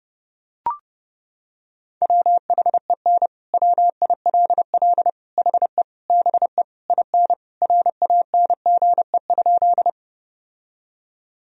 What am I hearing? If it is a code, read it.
Morse code, heard as a keyed tone, about 30 wpm, 700 Hz. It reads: WHEN WILL HE BE IN RANGE?
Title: Sentences from Top 1000 Words - Review Part 4 - 30wpm